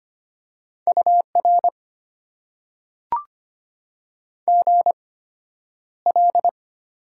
Morse code, heard as a keyed tone, about 25 wpm, spaced as for 11 wpm, 700 Hz. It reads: UR GL
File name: Top 90 QSO Elements - 8x Character Spacing - 25wpm